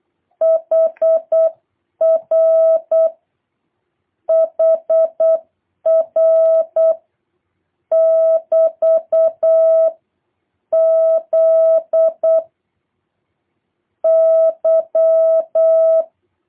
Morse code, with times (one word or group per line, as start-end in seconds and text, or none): 0.4-3.1 HR
4.3-6.9 HR
7.9-12.4 <BT>Z
14.0-16.0 Y